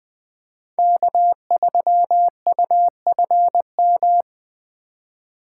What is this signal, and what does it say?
Morse code, heard as a keyed tone, about 23 wpm, 700 Hz. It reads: K3UFM
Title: Speed Racing - USA Call Signs 20wpm